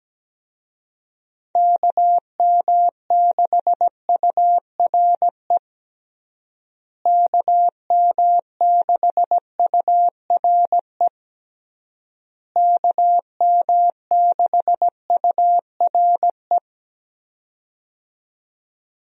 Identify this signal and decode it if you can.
Morse code, heard as a keyed tone, about 17 wpm, 700 Hz. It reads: KM6URE KM6URE KM6URE
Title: USA Call Signs - 6 Characters - Repeated 3x - 17wpm